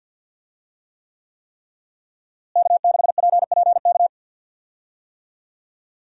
Morse code, heard as a keyed tone, about 47 wpm, 700 Hz. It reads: K6PPK
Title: USA Call Signs - 5 Characters - Repeated 3x - 50wpm